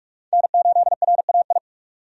Morse code, heard as a keyed tone, about 45 wpm, 700 Hz. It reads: N8RAI
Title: USA Call Signs - 5 Characters - 45wpm